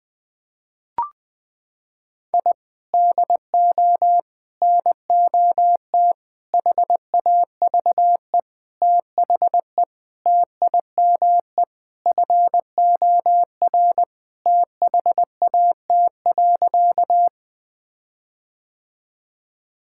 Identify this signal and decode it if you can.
Morse code, heard as a keyed tone, about 20 wpm, 700 Hz. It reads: I DO NOT HAVE THE TIME FOR THAT.
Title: Sentences with 5-Word Phrases - 20wpm